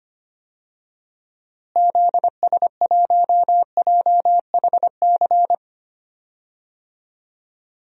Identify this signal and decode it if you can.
Morse code, heard as a keyed tone, about 25 wpm, 700 Hz. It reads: ZS1JHC
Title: DXCC-100 Call Signs - 6 Characters - Repeated 3x - 25wpm